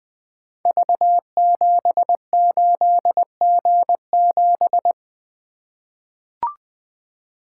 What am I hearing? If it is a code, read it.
Morse code, heard as a keyed tone, about 21 wpm, 700 Hz. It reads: V78G7
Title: Code Groups - 5 Letter:Numbers - 20wpm